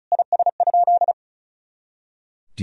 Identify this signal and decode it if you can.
Morse code, heard as a keyed tone, about 36 wpm, 700 Hz. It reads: IS?
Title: Sentences from Top 300 Words and Abbreviations 35wpm